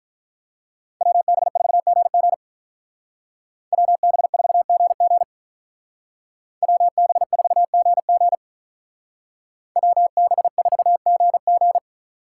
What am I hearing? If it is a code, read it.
Morse code, heard as a keyed tone, about 43 wpm, 700 Hz. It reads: WB4GG WB4GG WB4GG WB4GG
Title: Speed Racing - USA 2x2 Call Signs 35wpm